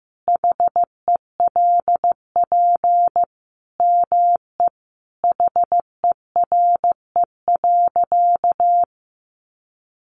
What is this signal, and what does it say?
Morse code, heard as a keyed tone, about 15 wpm, 700 Hz. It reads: HELP ME HERE.
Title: Sentences from Top 200 Words 15wpm